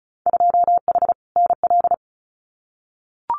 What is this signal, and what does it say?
Morse code, heard as a keyed tone, about 34 wpm, 700 Hz. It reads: 2H NL
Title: Winter Field Day Exchanges Encore 35wpm